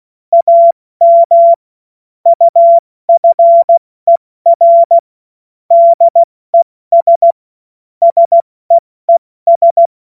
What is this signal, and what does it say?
Morse code, heard as a keyed tone, about 16 wpm, 680 Hz. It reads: AM UFER DES SEES
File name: Morsekurs / Witze / Witz 21. Februar 2023 70 F80